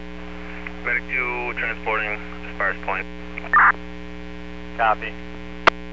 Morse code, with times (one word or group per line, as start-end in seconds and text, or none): none